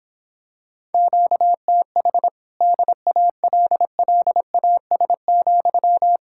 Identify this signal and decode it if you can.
Morse code, heard as a keyed tone, about 26 wpm, 700 Hz. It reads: QTH DALLAS,
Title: Speed-Racing Common QSO Phrases - 22wpm